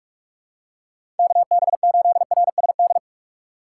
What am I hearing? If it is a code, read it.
Morse code, heard as a keyed tone, about 46 wpm, 700 Hz. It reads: KB8RSD
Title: USA Call Signs - 3x Character Spacing - 45wpm